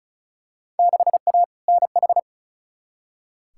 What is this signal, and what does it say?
Morse code, heard as a keyed tone, about 34 wpm, 700 Hz. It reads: 6A NH
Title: Field Day Exchanges - 35wpm